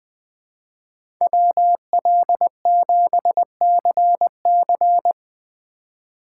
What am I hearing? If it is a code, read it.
Morse code, heard as a keyed tone, about 21 wpm, 700 Hz. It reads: WL7CC
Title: USA Call Signs - 5 Characters - 2x Character Spacing - 20wpm